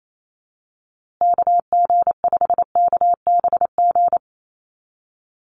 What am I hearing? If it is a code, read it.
Morse code, heard as a keyed tone, about 28 wpm, 700 Hz. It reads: KG5KBG
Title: USA Call Signs Repeated 3x 28wpm